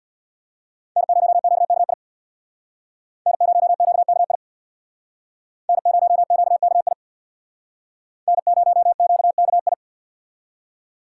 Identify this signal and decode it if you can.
Morse code, heard as a keyed tone, about 63 wpm, 690 Hz. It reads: N0QKI N0QKI N0QKI N0QKI